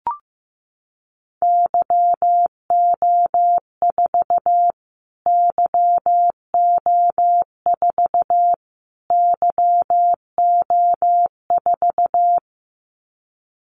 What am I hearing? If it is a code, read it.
Morse code, heard as a keyed tone, about 15 wpm, 700 Hz. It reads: YO4 YO4 YO4
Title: Callsign Prefixes for Top 25 DXCC - Repeated 3x - 15wpm